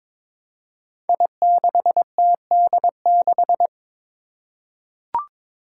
Morse code, heard as a keyed tone, about 22 wpm, 700 Hz. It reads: I6TD6
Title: Five Challenging Letter-Number Combinations 22wpm